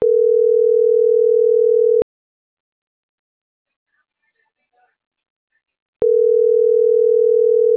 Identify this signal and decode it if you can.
Ringback tone (North America), 440 + 480 Hz.